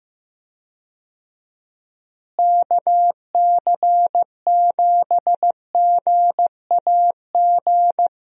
Morse code, heard as a keyed tone, about 15 wpm, 700 Hz. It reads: KC7GAG
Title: USA Call Signs - 6 Characters - Repeated 2x - 15wpm